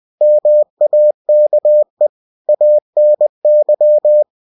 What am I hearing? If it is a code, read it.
Morse code, heard as a keyed tone, about 20 wpm, 600 Hz. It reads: MAKE ANY